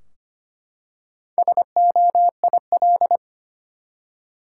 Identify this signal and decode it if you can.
Morse code, heard as a keyed tone, about 25 wpm, 700 Hz. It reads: SOIL
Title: Four Letter Words - 25wpm